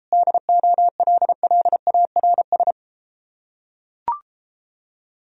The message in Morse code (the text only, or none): DOLLARS